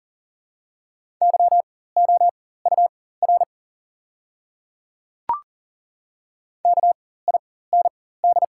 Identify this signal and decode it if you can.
Morse code, heard as a keyed tone, about 40 wpm, spaced as for 19 wpm, 700 Hz. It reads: YOUR KIND
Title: 4x Character Spacing - Top 200 Words - 40wpm